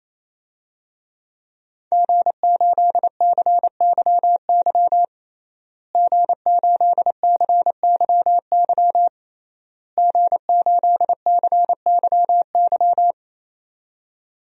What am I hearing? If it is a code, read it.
Morse code, heard as a keyed tone, about 27 wpm, 700 Hz. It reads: G8CYY G8CYY G8CYY
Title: DXCC-100 Call Signs - 5 Characters - Repeated 3x - 28wpm